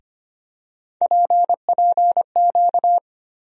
Morse code, heard as a keyed tone, about 25 wpm, 700 Hz. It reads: PPQ
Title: USA Call Sign Postfixes - 3 Letters 25wpm